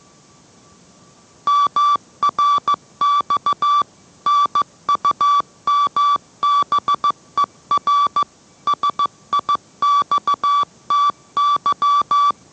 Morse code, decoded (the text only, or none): MRX NUMBER SIXTY